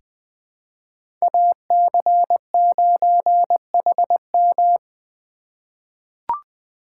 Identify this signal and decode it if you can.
Morse code, heard as a keyed tone, about 20 wpm, 700 Hz. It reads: AC9HM
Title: USA Call Signs - 5 Characters - 4x Character Spacing - 20wpm